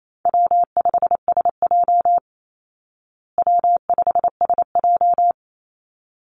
Morse code, heard as a keyed tone, about 28 wpm, 700 Hz. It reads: W5SJ W5SJ
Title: Call Signs Repeated 3x - 28wpm